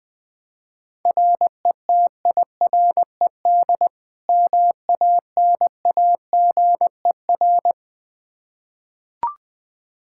Morse code, heard as a keyed tone, about 20 wpm, 700 Hz. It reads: RETIRED MANAGER
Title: K1USN SST QSO Practice 20wpm